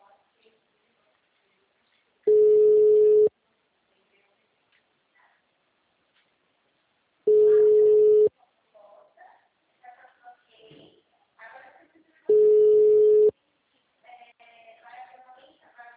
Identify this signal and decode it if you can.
Ringback tone (Europe, 425 Hz), with speech.